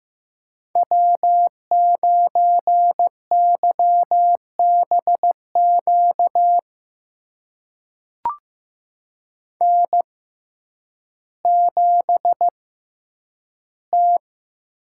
Morse code, heard as a keyed tone, about 15 wpm, spaced as for 9 wpm, 700 Hz. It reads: W9YBQ N7T
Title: USA Call Signs - 5 Characters - 6x Character Spacing - 15wpm